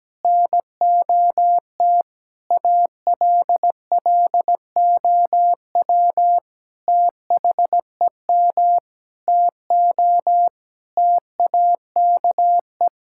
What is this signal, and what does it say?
Morse code, heard as a keyed tone, about 17 wpm, 700 Hz. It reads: NOT ALLOW THEM TO TAKE
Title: Sentences from Top 1000 Words - Review Part 5 - 17wpm